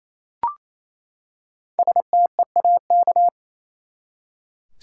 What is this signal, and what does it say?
Morse code, heard as a keyed tone, about 28 wpm, 700 Hz. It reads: STEAK
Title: Five Letter Words - 28wpm